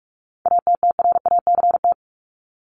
Morse code, heard as a keyed tone, about 44 wpm, 700 Hz. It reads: ATTRACT
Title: Top 1501-1750 Words 45wpm